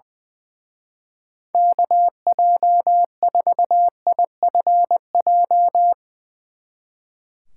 Morse code, heard as a keyed tone, about 22 wpm, 700 Hz. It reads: KJ4IFJ
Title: Speed Racing - USA 2x3 Call Signs 20wpm